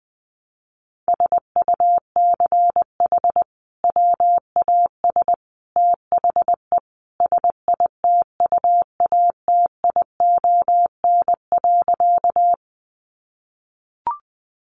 Morse code, heard as a keyed tone, about 20 wpm, 700 Hz. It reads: SUCH WAS THE SITUATION.
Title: Sentences from Top 1200 Words 20wpm